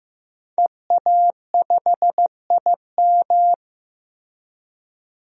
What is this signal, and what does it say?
Morse code, heard as a keyed tone, about 15 wpm, 700 Hz. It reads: EA5IM
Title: DXCC-100 Call Signs - 5 Characters - 15wpm